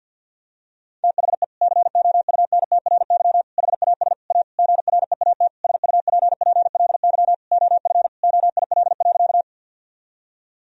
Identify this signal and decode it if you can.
Morse code, heard as a keyed tone, about 50 wpm, 700 Hz. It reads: THE COUNTRY HAS A GREAT SUPPLY OF OIL.
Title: Sentences from Top 1000 Words - Review Part 4 - 50wpm